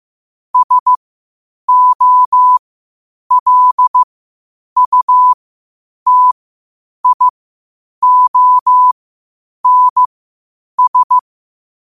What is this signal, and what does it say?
Morse code, heard as a keyed tone, about 15 wpm, spaced as for 8 wpm, 1000 Hz. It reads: SOLUTIONS
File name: qotdfiles / 8wpm